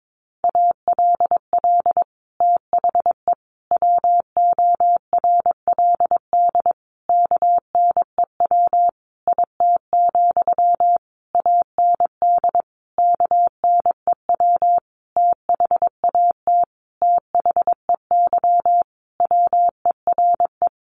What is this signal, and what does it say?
Morse code, heard as a keyed tone, about 22 wpm, 700 Hz. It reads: ALL THE WORLD KNEW IT, AND KNEW THAT THEY WERE
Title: Sentences from Top 1200 Words 22wpm